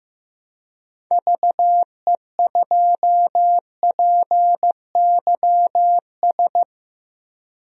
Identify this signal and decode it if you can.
Morse code, heard as a keyed tone, about 15 wpm, 700 Hz. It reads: VE2PYS
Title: DXCC-100 Call Signs - 6 Characters - Repeated 3x - 15wpm